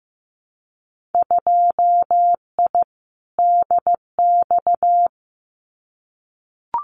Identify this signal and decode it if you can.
Morse code, heard as a keyed tone, about 15 wpm, 700 Hz. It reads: 2I DX
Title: Winter Field Day Exchanges 15wpm